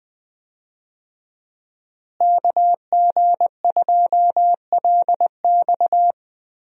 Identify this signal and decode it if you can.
Morse code, heard as a keyed tone, about 20 wpm, 700 Hz. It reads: KG2LX
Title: USA Call Signs - 5 Characters - Repeated 3x - 20wpm